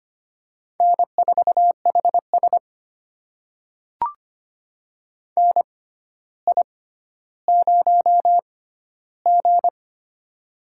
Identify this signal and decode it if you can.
Morse code, heard as a keyed tone, about 25 wpm, spaced as for 13 wpm, 700 Hz. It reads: N4HS NI0G
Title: USA Call Signs - 4 Characters - 6x Character Spacing - 25wpm